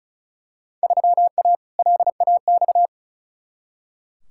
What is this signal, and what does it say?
Morse code, heard as a keyed tone, about 36 wpm, 700 Hz. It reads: 3A LAX
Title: Field Day Exchanges - 35wpm